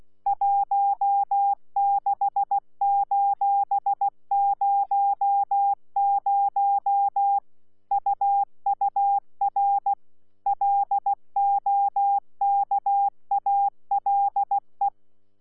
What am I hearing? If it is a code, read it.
Morse code, heard as a keyed tone, about 16 wpm, 800 Hz. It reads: 16:00 UUR LOKALE
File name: Z PI4VRZ / PI4VRZ 14-jun-2025